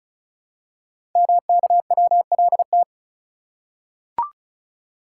Morse code, heard as a keyed tone, about 36 wpm, 700 Hz. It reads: MKWLT
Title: Code Groups - 5 Letter:Numbers - 35wpm